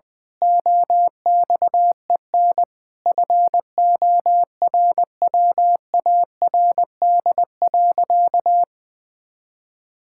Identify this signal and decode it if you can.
Morse code, heard as a keyed tone, about 20 wpm, 700 Hz. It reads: OXEN FORWARD.
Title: Pangram Challenge v2 - 20wpm